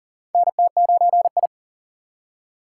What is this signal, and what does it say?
Morse code, heard as a keyed tone, about 40 wpm, 690 Hz. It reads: NT9I
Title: USA Call Signs - 3x Character Spacing - 40wpm